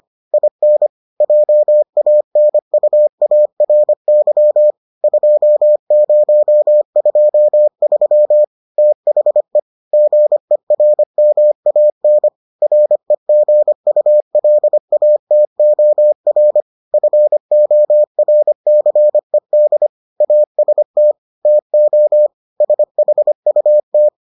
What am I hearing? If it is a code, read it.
Morse code, heard as a keyed tone, about 25 wpm, 600 Hz. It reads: IN JANUARY 2023 THE GERMAN REGULATOR FORCED AST TO SHUT